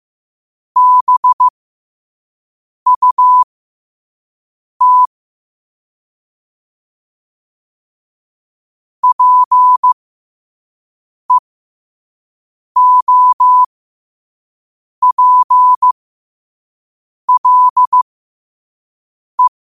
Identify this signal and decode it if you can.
Morse code, heard as a keyed tone, about 15 wpm, spaced as for 5 wpm, 1000 Hz. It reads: BUT PEOPLE
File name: 5wpm